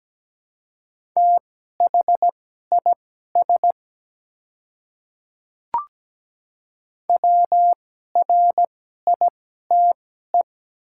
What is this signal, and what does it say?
Morse code, heard as a keyed tone, about 17 wpm, spaced as for 12 wpm, 700 Hz. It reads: THIS WRITE
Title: 2x Character Spacing - Top 100 Words - 17wpm